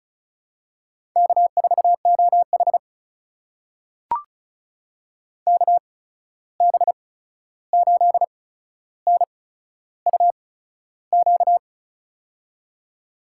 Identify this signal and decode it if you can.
Morse code, heard as a keyed tone, about 36 wpm, spaced as for 13 wpm, 700 Hz. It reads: K4OH KB8NUQ
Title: USA Call Signs - 8x Character Spacing - 35wpm